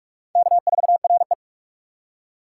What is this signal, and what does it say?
Morse code, heard as a keyed tone, about 41 wpm, 700 Hz. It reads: K4RE
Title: USA Call Signs - 4 Characters - 4x Character Spacing - 45wpm